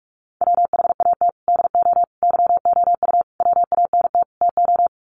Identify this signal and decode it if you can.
Morse code, heard as a keyed tone, about 44 wpm, 700 Hz. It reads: WHAT DO YOU WANT TO
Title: Sets of 5 Words 45wpm